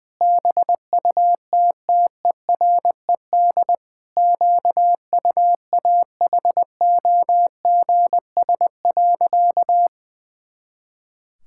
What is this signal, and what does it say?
Morse code, heard as a keyed tone, about 20 wpm, 700 Hz. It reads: BUTTERED QUAHOGS.
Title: Pangram Challenge v2 - 20wpm